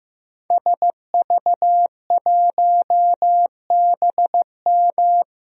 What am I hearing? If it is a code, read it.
Morse code, heard as a keyed tone, about 15 wpm, 700 Hz. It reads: SV1BM
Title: DXCC-100 Call Signs - 5 Characters - Repeated 3x - 15wpm